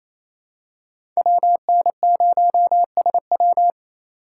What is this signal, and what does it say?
Morse code, heard as a keyed tone, about 28 wpm, 700 Hz. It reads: WN0SW
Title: USA Call Signs - 5 Characters - 4x Character Spacing - 28wpm